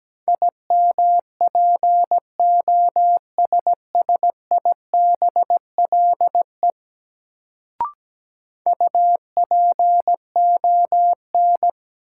Morse code, heard as a keyed tone, about 17 wpm, 700 Hz. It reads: IMPOSSIBLE UPON
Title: Sentences from Top 1100 Words - New Word Drill Alt - 17wpm